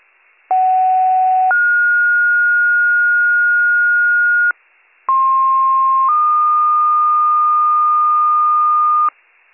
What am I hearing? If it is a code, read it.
Two-tone page, heard as a one-second tone, then a three-second tone, about 750 then 1430 Hz, about 1060 then 1220 Hz, with noise. It is heard from 0.5 to 9.1 s.